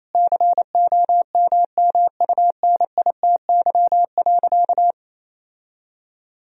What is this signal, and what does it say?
Morse code, heard as a keyed tone, about 28 wpm, 700 Hz. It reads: COMMUNITY.
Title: Sentences from Top 1000 Words - Review Part 4 - 28wpm